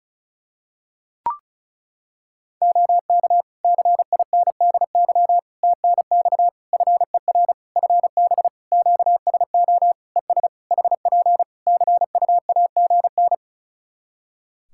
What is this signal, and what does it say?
Morse code, heard as a keyed tone, about 34 wpm, 700 Hz. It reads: OK CINDY TNX FER FB QSO ES HP CUAGN
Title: K1USN SST QSO Practice 35wpm